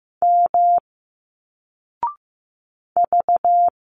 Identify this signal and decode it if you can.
Morse code, heard as a keyed tone, about 15 wpm, 700 Hz. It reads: M V
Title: Single Letters 15wpm